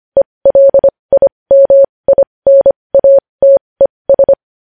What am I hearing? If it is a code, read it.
Morse code, heard as a keyed tone, about 25 wpm, spaced as for 20 wpm, 550 Hz. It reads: ELIMINATES